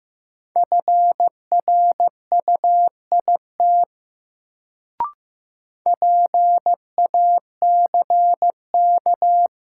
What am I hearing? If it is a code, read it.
Morse code, heard as a keyed tone, about 15 wpm, 700 Hz. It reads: FRUIT PACK
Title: Sentences from Top 1300 Words - New Word Drill - 15wpm